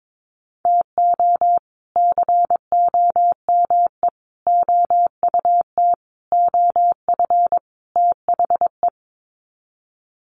Morse code, heard as a keyed tone, about 22 wpm, 700 Hz. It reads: TO COME OUT OF THE
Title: Sets of 5 Words Encore2 22wpm